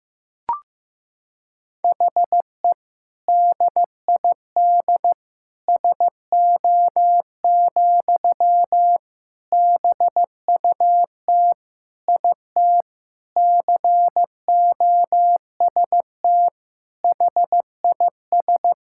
Morse code, heard as a keyed tone, about 15 wpm, 700 Hz. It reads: HE DID SO, BUT IT COST HIS
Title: Sentences from Top 1000 Words - Review Part 5 - 15wpm